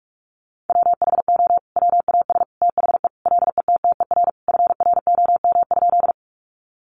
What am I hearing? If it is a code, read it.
Morse code, heard as a keyed tone, about 44 wpm, 700 Hz. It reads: WHO WAS THE LETTER FROM?
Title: Sentences from Top 300 Words 45wpm